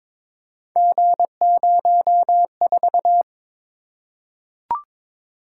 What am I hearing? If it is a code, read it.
Morse code, heard as a keyed tone, about 22 wpm, 700 Hz. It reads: G04
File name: practice / 22wpm / Three Letter-Number Combinations - 22wpm